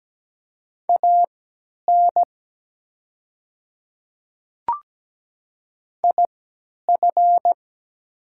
Morse code, heard as a keyed tone, about 17 wpm, spaced as for 10 wpm, 700 Hz. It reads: AN IF
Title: 3x Character Spacing - Top 100 Words - 17wpm